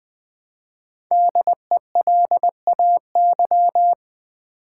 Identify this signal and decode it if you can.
Morse code, heard as a keyed tone, about 20 wpm, 700 Hz. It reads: DELAY
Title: Five Letter Words - 20wpm